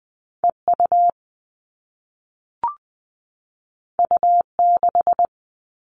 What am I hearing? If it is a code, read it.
Morse code, heard as a keyed tone, about 20 wpm, 700 Hz. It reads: EU U6
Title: All Cut Numbers - 2-Digit 20wpm